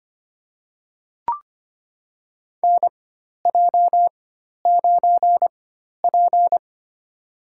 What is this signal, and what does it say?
Morse code, heard as a keyed tone, about 25 wpm, spaced as for 12 wpm, 700 Hz. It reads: NJ9P
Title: USA Call Signs - 4 Characters - 4x Character Spacing - 25wpm